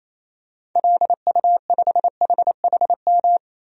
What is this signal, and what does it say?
Morse code, heard as a keyed tone, about 28 wpm, 700 Hz. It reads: LU5HHM
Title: DXCC-100 Call Signs - 6 Characters - 28wpm